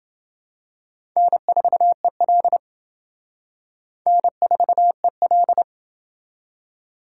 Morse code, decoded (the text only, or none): N4EL N4EL